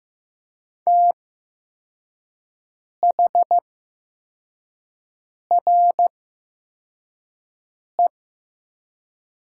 Morse code, heard as a keyed tone, about 15 wpm, spaced as for 4 wpm, 700 Hz. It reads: THRE
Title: 8x Character Spacing - Top 300 Words - 15wpm